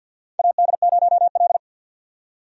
Morse code, heard as a keyed tone, about 47 wpm, 700 Hz. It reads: AD0L
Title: USA Call Signs - 4 Characters - 50wpm